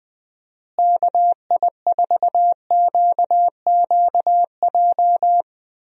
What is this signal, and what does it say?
Morse code, heard as a keyed tone, about 20 wpm, 700 Hz. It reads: KI4QQJ
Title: USA Call Signs - 6 Characters - Repeated 3x - 20wpm